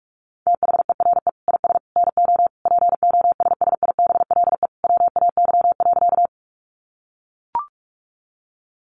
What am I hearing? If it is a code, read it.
Morse code, heard as a keyed tone, about 44 wpm, 700 Hz. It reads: THERE IS NO POSSIBLE WAY.
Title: Sentences from Top 500 Words 45wpm